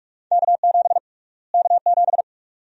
Morse code, heard as a keyed tone, about 45 wpm, 700 Hz. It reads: K7 K7
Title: USA Call Sign Prefixes - Repeated 2x - 45wpm